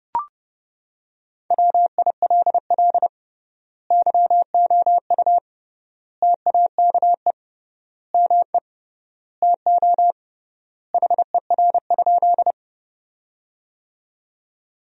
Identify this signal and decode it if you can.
Morse code, heard as a keyed tone, about 29 wpm, 700 Hz. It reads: WILL YOU TAKE ME TO HER?
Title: 3x Word-Spacing - Sentences from Top 200 Words - 30wpm